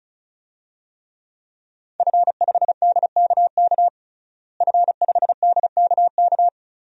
Morse code, heard as a keyed tone, about 34 wpm, 700 Hz. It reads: F5DKK F5DKK
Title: DXCC-100 Call Signs - 5 Characters - Repeated 3x - 35wpm